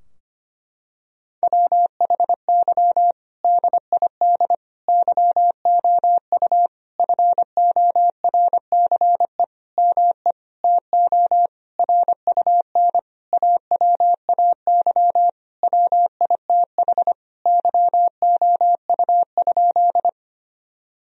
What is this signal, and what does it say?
Morse code, heard as a keyed tone, about 25 wpm, 700 Hz. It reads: WHY DID YOU FORCE ME TO RUN AWAY WITH YOU?
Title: Sentences from Top 1000 Words - Review Part 3 - 25wpm